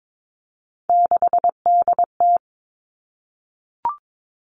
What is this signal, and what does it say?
Morse code, heard as a keyed tone, about 22 wpm, 700 Hz. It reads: 6DT
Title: Three Challenging Letter-Number Combinations 22wpm